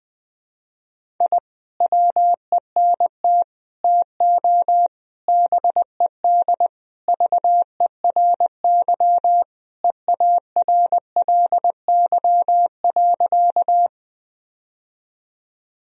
Morse code, heard as a keyed tone, about 20 wpm, 700 Hz. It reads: I WENT TO BED VERY EARLY.
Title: Sentences from Top 1000 Words - Review Part 2 - 20wpm